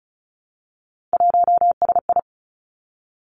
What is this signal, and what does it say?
Morse code, heard as a keyed tone, about 36 wpm, 700 Hz. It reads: 1SI
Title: Three Challenging Letter-Number Combinations 35wpm